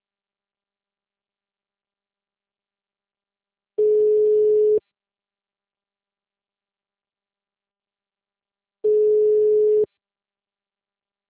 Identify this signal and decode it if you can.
Ringback tone (Europe, 425 Hz).
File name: out- -2045- - - .3323